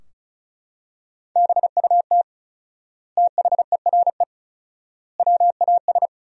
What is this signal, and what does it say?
Morse code, heard as a keyed tone, about 36 wpm, 700 Hz. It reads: BUT THERE WAS
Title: 4x Word-Spacing - Sentences from Top 100 Words - 35wpm